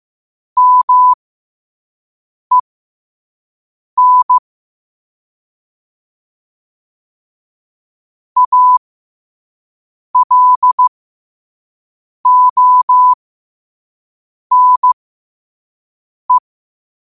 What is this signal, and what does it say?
Morse code, heard as a keyed tone, about 15 wpm, spaced as for 5 wpm, 1000 Hz. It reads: MEN ALONE
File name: 5wpm